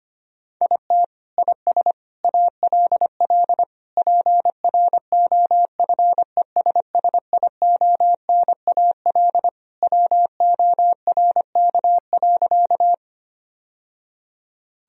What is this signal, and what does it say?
Morse code, heard as a keyed tone, about 25 wpm, 700 Hz. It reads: IT IS ALL PROFESSIONAL WORK.
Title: Sentences from Top 1000 Words - Review Part 4 - 25wpm